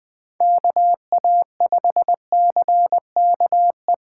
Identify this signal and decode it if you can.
Morse code, heard as a keyed tone, about 20 wpm, 700 Hz. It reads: KA5CKE
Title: Speed Racing - USA Call Signs 17wpm